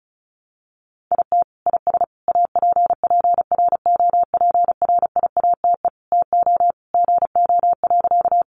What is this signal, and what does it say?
Morse code, heard as a keyed tone, about 34 wpm, 700 Hz. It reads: IT IS APPROPRIATE TO GO.
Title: Sentences from Top 900 Words 35wpm